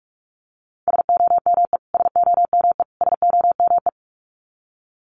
Morse code, heard as a keyed tone, about 44 wpm, 700 Hz. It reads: SOME SOME SOME
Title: Top 200 Words Repeated 45wpm